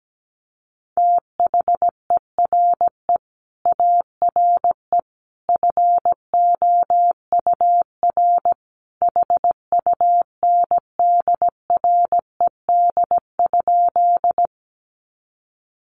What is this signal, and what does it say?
Morse code, heard as a keyed tone, about 17 wpm, 700 Hz. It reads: THERE ARE FOUR HUNDRED?
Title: Sentences from Top 400 Words 17wpm